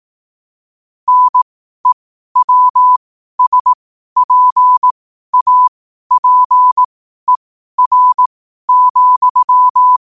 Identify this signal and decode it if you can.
Morse code, heard as a keyed tone, about 18 wpm, spaced as for 12 wpm, 1000 Hz. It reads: NEWSPAPER,
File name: qotdfiles / 12wpm